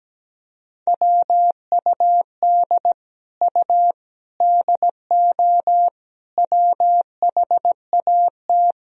Morse code, heard as a keyed tone, about 17 wpm, 700 Hz. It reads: WUD U DO WHAT
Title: Sentences from Top 100 Words and Abbreviations 17wpm